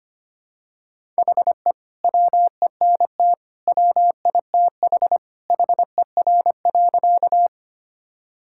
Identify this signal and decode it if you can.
Morse code, heard as a keyed tone, about 25 wpm, 700 Hz. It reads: HE WENT WITH HER.
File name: practice / 25wpm / Sentences with 2-Word Phrases - 25wpm